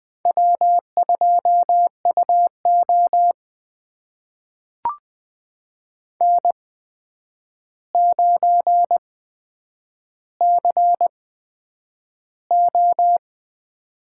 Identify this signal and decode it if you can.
Morse code, heard as a keyed tone, about 20 wpm, spaced as for 9 wpm, 700 Hz. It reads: W2UO N9CO
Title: USA Call Signs - 4 Characters - 8x Character Spacing - 20wpm